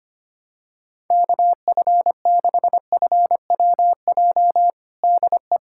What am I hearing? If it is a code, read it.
Morse code, heard as a keyed tone, about 25 wpm, 700 Hz. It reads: KF6FWJ DE
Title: USA Callsign DE Callsign - 6 Characters - 25wpm